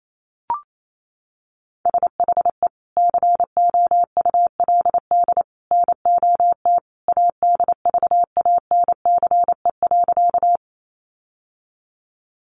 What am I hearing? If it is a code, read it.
Morse code, heard as a keyed tone, about 28 wpm, 700 Hz. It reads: SHE COULD NOT ADVANCE.
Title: Sentences from Top 1400 Words 28wpm